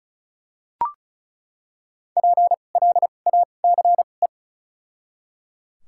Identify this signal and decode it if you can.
Morse code, heard as a keyed tone, about 34 wpm, spaced as for 25 wpm, 700 Hz. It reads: PLACE
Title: 2x Character Spacing - Top 500 Words - 35wpm